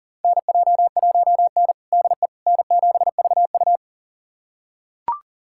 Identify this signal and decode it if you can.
Morse code, heard as a keyed tone, about 40 wpm, 700 Hz. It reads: NJ1N DE N7VU
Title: USA Callsign DE Callsign - 4 Characters - 40wpm